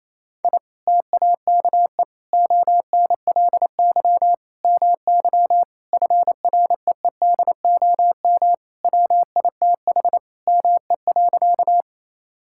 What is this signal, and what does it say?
Morse code, heard as a keyed tone, about 28 wpm, 700 Hz. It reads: I TAKE ONLY MY FREEDOM WITH ME.
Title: Sentences from Top 1000 Words - Review Part 5 - 28wpm